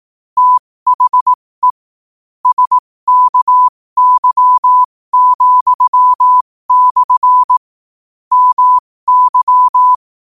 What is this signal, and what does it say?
Morse code, heard as a keyed tone, about 18 wpm, spaced as for 15 wpm, 1000 Hz. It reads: THE SKY,/ MY
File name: qotdfiles / 15wpm